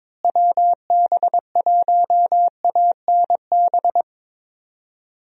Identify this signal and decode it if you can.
Morse code, heard as a keyed tone, about 22 wpm, 700 Hz. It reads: WB1ANB
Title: USA Call Signs - 6 Characters - Repeated 3x - 22wpm